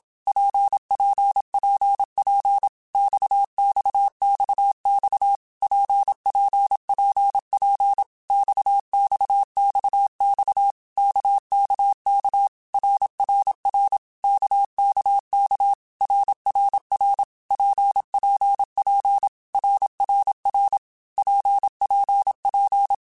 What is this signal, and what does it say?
Morse code, heard as a keyed tone, about 27 wpm, 800 Hz. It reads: PPPP XXXX PPPP XXXX KKK RRR KKK RRR PPP RRR PPP
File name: Basic Letter Groups By7Groups / RKPX / RKPX@25